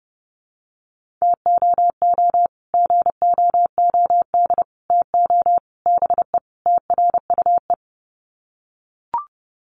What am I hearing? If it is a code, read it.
Morse code, heard as a keyed tone, about 30 wpm, 700 Hz. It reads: TOO GOOD TO BE TRUE
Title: Sets of 5 Words Encore 30wpm